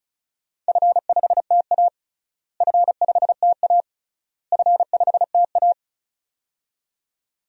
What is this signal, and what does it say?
Morse code, heard as a keyed tone, about 36 wpm, 700 Hz. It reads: F5TA F5TA F5TA
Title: DXCC-100 Call Signs - 4 Characters - Repeated 3x - 35wpm